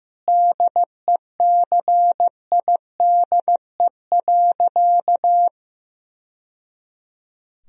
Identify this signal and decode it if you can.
Morse code, heard as a keyed tone, about 15 wpm, 700 Hz. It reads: DECIDE.
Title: Sentences from Top 1000 Words - Review Part 3 - 15wpm